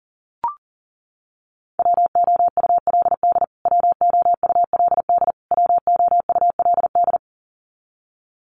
Morse code, heard as a keyed tone, about 40 wpm, 700 Hz. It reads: WOULD WOULD WOULD
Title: Top 500 Words Repeated 40wpm